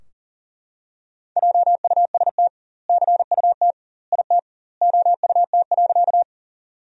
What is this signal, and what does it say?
Morse code, heard as a keyed tone, about 40 wpm, 700 Hz. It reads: JUST CUT IT OUT.
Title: 2x Word-Spacing - Sentences from Top 300 Words - 40wpm